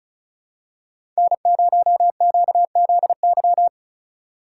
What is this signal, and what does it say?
Morse code, heard as a keyed tone, about 36 wpm, 700 Hz. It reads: N0QZY
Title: USA Call Signs - 3x Character Spacing - 35wpm